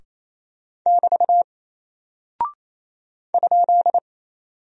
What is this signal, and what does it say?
Morse code, heard as a keyed tone, about 28 wpm, 700 Hz. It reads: <BT> ?
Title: Single Character (Letter:Number:Punctuation:Prosign) 28wpm